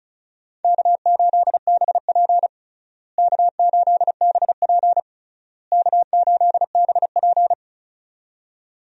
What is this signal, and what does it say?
Morse code, heard as a keyed tone, about 34 wpm, 700 Hz. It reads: K8BP K8BP K8BP
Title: USA Call Signs - 4 Characters - Repeated 3x - 35wpm